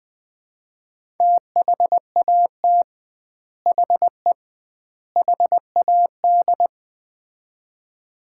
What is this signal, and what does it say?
Morse code, heard as a keyed tone, about 20 wpm, 700 Hz. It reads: THAT HE HAD
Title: 2x Word-Spacing - Sets of 3 Words - 20wpm